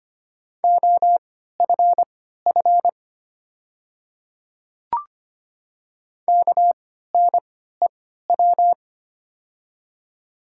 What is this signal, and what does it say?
Morse code, heard as a keyed tone, about 25 wpm, spaced as for 14 wpm, 690 Hz. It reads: OFF KNEW
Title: 3x Character Spacing - Top 500 Words - 25wpm